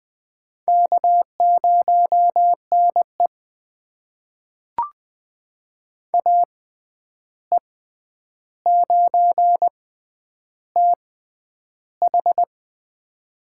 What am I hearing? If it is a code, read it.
Morse code, heard as a keyed tone, about 20 wpm, spaced as for 10 wpm, 700 Hz. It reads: K0NE AE9TH
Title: USA Call Signs - 6x Character Spacing - 20wpm